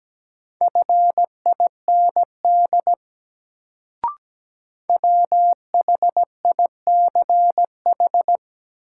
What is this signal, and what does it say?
Morse code, heard as a keyed tone, about 17 wpm, 700 Hz. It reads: FIND WHICH
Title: Sentences from Top 100 Words - New Word Drill - 17wpm